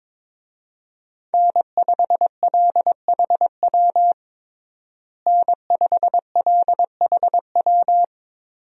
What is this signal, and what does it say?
Morse code, heard as a keyed tone, about 21 wpm, 700 Hz. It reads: N5LHW N5LHW
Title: USA Call Signs - 5 Characters - Repeated 3x - 22wpm